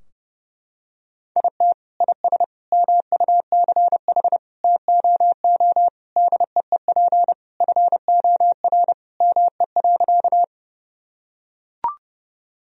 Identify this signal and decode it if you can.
Morse code, heard as a keyed tone, about 32 wpm, 700 Hz. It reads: IT IS MUCH TOO DEEP FOR ME.
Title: Sentences from Top 1000 Words - Review Part 3 - 30wpm